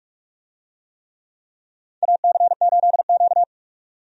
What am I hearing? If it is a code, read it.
Morse code, heard as a keyed tone, about 41 wpm, 700 Hz. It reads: AC8Q
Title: USA Call Signs - 4 Characters - 45wpm